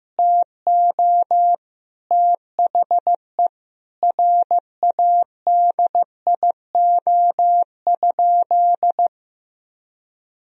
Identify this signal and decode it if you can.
Morse code, heard as a keyed tone, about 15 wpm, 700 Hz. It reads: TO THE RADIO?
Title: Sentences from Top 1000 Words - Review Part 3 - 15wpm